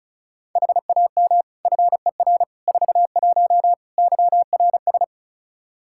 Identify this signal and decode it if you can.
Morse code, heard as a keyed tone, about 34 wpm, 700 Hz. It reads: HAM FER 41 YRS
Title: Practice QSO - Exchange 3 - More Info - 35wpm